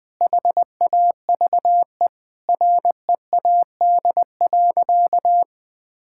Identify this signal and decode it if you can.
Morse code, heard as a keyed tone, about 20 wpm, 700 Hz. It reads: HAVE READ.
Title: Sentences with 5-Word Phrases - 20wpm